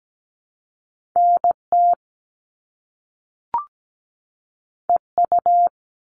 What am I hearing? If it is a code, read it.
Morse code, heard as a keyed tone, about 17 wpm, 700 Hz. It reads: NT EU
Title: Top Two Letter Combinations in Words 17wpm